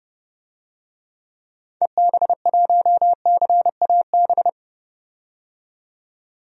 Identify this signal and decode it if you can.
Morse code, heard as a keyed tone, about 30 wpm, 700 Hz. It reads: EB1CAB
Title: DXCC-100 Call Signs - 6 Characters - 30wpm